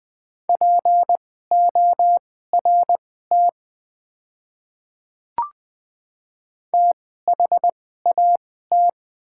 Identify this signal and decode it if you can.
Morse code, heard as a keyed tone, about 20 wpm, spaced as for 14 wpm, 700 Hz. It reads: PORT THAT
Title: 2x Character Spacing - Top 200 Words - 20wpm